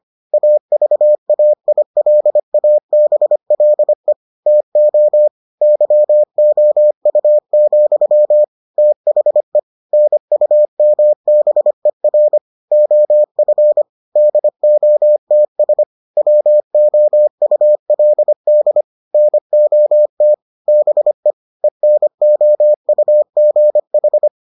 Morse code, heard as a keyed tone, about 25 wpm, 600 Hz. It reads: AVAILABLE TO YOU, THE NUMBER OF DOTS WOULD NOT BE ENOUGH